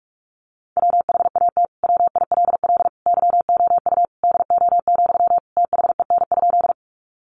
Morse code, heard as a keyed tone, about 44 wpm, 700 Hz. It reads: WHAT WILL YOU DO, THEN?
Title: Sentences from Top 100 Words 45wpm